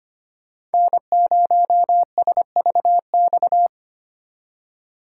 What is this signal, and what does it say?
Morse code, heard as a keyed tone, about 31 wpm, 700 Hz. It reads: N0SVX
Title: Speed Racing - USA 1x3 Call Signs 25wpm